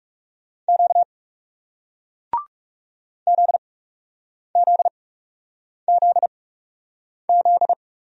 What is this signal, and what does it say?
Morse code, heard as a keyed tone, about 38 wpm, spaced as for 8 wpm, 700 Hz. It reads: Q ZZZZ